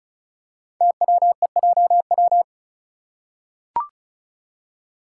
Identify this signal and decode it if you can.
Morse code, heard as a keyed tone, about 36 wpm, 690 Hz. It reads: TWEJW
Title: Code Groups - 5 Letters - 35wpm